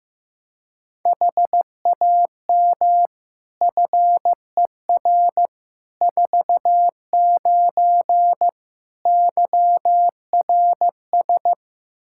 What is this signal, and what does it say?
Morse code, heard as a keyed tone, about 15 wpm, 700 Hz. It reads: HAM FER 49 YRS
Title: K1USN SST QSO Practice 15wpm